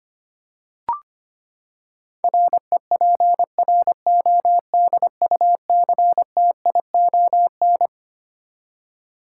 Morse code, heard as a keyed tone, about 25 wpm, 700 Hz. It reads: REPRODUCTION
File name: Twelve Letter Words - 25wpm